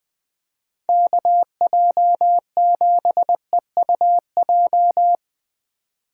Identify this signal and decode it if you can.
Morse code, heard as a keyed tone, about 20 wpm, 700 Hz. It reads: KJ7EUJ